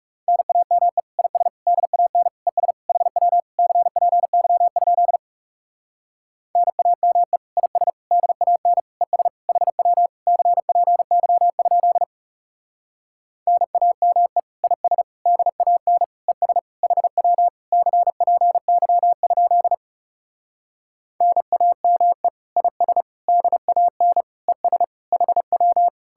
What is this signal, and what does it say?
Morse code, heard as a keyed tone, about 40 wpm, 700 Hz. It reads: NAME IS DAN ES HW CPY? NAME IS DAN ES HW CPY? NAME IS DAN ES HW CPY? NAME IS DAN ES HW